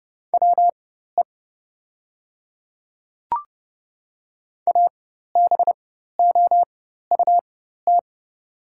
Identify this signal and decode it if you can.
Morse code, heard as a keyed tone, about 32 wpm, spaced as for 14 wpm, 700 Hz. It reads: WE ABOUT
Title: 4x Character Spacing - Top 100 Words - 30wpm